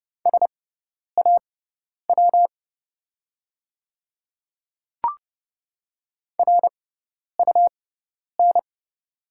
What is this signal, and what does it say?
Morse code, heard as a keyed tone, about 30 wpm, spaced as for 10 wpm, 700 Hz. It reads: SAW RUN
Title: 6x Character Spacing - Top 500 Words - 30wpm